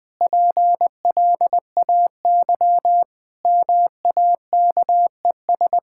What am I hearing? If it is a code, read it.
Morse code, heard as a keyed tone, about 20 wpm, 700 Hz. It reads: PLAY MAKES
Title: Sentences with 8-Word Phrases - 20wpm